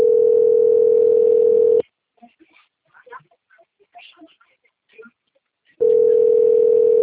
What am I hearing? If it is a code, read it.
Ringback tone (North America), 440 + 480 Hz.